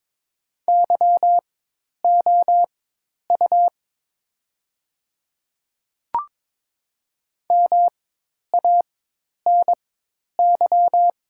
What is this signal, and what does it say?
Morse code, heard as a keyed tone, about 22 wpm, spaced as for 10 wpm, 700 Hz. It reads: YOU MANY
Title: 4x Character Spacing - Top 300 Words - 22wpm